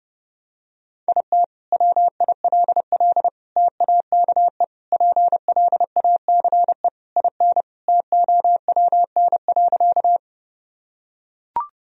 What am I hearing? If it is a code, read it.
Morse code, heard as a keyed tone, about 30 wpm, 700 Hz. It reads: IT WILL TAKE PLACE IN TOWN.
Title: Sentences from Top 1000 Words - Review Part 2 - 30wpm